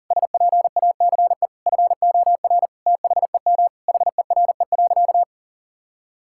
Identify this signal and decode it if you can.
Morse code, heard as a keyed tone, about 40 wpm, 700 Hz. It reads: SPACE FOR THEM HERE.
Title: Sentences from Top 1000 Words - Review Part 5 - 40wpm